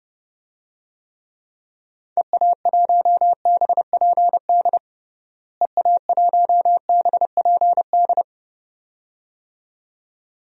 Morse code, heard as a keyed tone, about 30 wpm, 700 Hz. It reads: EA1BPD EA1BPD
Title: DXCC-100 Call Signs - 6 Characters - Repeated 2x - 30wpm